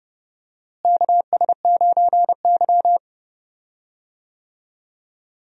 Morse code, heard as a keyed tone, about 30 wpm, 700 Hz. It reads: KS9Y
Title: USA Call Signs - 4 Characters - Repeated 3x - 30wpm